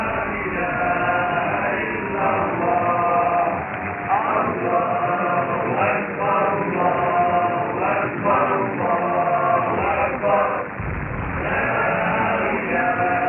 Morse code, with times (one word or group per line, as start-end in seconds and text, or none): none